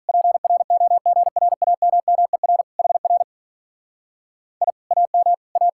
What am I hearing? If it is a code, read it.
Morse code, heard as a keyed tone, about 45 wpm, 700 Hz. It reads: PROGRAMMER HR I AM A